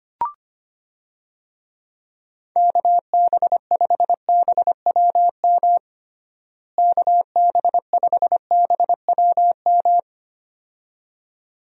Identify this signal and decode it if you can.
Morse code, heard as a keyed tone, about 25 wpm, 700 Hz. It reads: KB5BWM KB5BWM
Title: USA Call Signs - 6 Characters - Repeated 2x - 25wpm